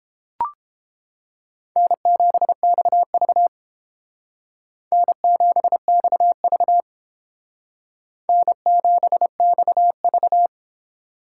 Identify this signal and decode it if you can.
Morse code, heard as a keyed tone, about 29 wpm, 700 Hz. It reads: N7XV N7XV N7XV